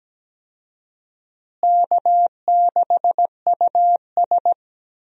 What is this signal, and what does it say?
Morse code, heard as a keyed tone, about 17 wpm, 700 Hz. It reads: K6US
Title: USA Call Signs - 4 Characters - Repeated 2x - 17wpm